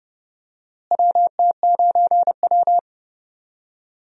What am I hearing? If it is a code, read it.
Morse code, heard as a keyed tone, about 29 wpm, 700 Hz. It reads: WT9W
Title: USA Call Signs - 4 Characters - Repeated 3x - 30wpm